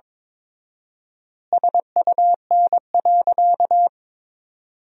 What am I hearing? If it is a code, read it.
Morse code, heard as a keyed tone, about 21 wpm, 700 Hz. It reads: SUN.
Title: 4x Word-Spacing - Sentences from Top 300 Words - 22wpm